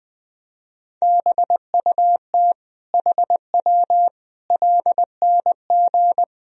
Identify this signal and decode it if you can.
Morse code, heard as a keyed tone, about 20 wpm, 700 Hz. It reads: BUT HW LNG
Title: Sentences from Top 500 Words and Abbreviations 20wpm